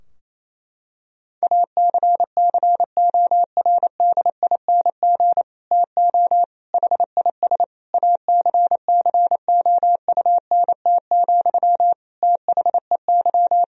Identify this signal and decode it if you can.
Morse code, heard as a keyed tone, about 28 wpm, 700 Hz. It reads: ACCORDING TO HIS ACCOUNT, THEY
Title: Sentences from Top 1000 Words - Review Part 4 - 28wpm